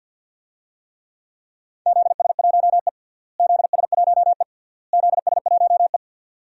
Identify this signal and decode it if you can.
Morse code, heard as a keyed tone, about 49 wpm, 700 Hz. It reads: ZS1E ZS1E ZS1E
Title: DXCC-100 Call Signs - 4 Characters - Repeated 3x - 50wpm